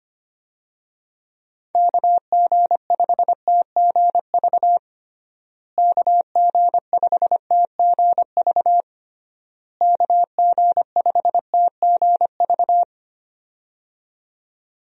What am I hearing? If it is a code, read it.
Morse code, heard as a keyed tone, about 25 wpm, 700 Hz. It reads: KG5TGV KG5TGV KG5TGV
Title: USA Call Signs - 6 Characters - Repeated 3x - 25wpm